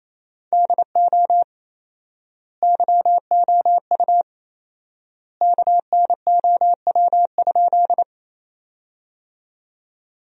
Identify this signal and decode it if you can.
Morse code, heard as a keyed tone, about 27 wpm, 700 Hz. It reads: DO YOU KNOW?
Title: 4x Word-Spacing - Sentences from Top 100 Words - 28wpm